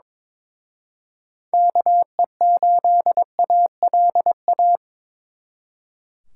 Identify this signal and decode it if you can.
Morse code, heard as a keyed tone, about 25 wpm, 700 Hz. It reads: KE8ALA
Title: Speed Racing - USA 2x3 Call Signs 22wpm